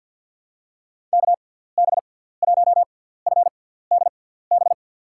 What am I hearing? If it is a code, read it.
Morse code, heard as a keyed tone, about 53 wpm, 700 Hz. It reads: K B 1 F D B
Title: USA Call Signs - 6 Characters - 6x Character Spacing - 50wpm